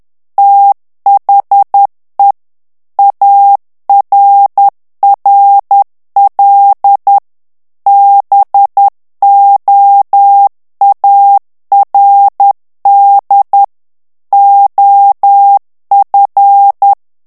Morse code, heard as a keyed tone, about 10 wpm, 800 Hz. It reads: THE ARRL BOARD OF
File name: ANGLAIS texte clair 3 vitesse 10 mots minute